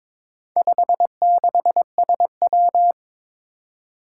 Morse code, heard as a keyed tone, about 22 wpm, 700 Hz. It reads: 56SW
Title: Four Challenging Letter-Number Combinations 22wpm